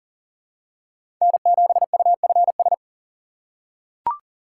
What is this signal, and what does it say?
Morse code, heard as a keyed tone, about 40 wpm, 700 Hz. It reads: N7UFS
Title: USA Call Signs - 5 Characters - 2x Character Spacing - 40wpm